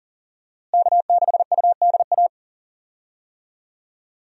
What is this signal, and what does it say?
Morse code, heard as a keyed tone, about 38 wpm, 700 Hz. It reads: K6UDA
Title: USA Call Signs - 5 Characters - Repeated 2x - 40wpm